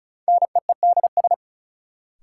Morse code, heard as a keyed tone, about 36 wpm, 700 Hz. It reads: NEEDS
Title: Sentences from Top 1000 Words - Review Part 3 - New Word Drill - 35wpm